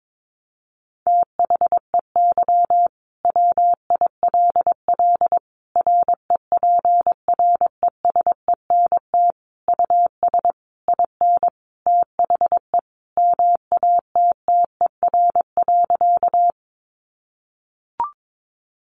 Morse code, heard as a keyed tone, about 22 wpm, 700 Hz. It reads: THEY WILL REPRESENT US IN THE MATTER.
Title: Sentences from Top 1200 Words 22wpm